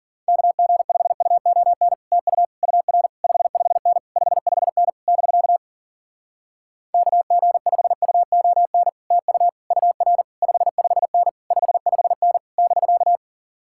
Morse code, heard as a keyed tone, about 52 wpm, 700 Hz. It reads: KG5UON TU UR 55N 55N <BK> KG5UON TU UR 55N 55N <BK>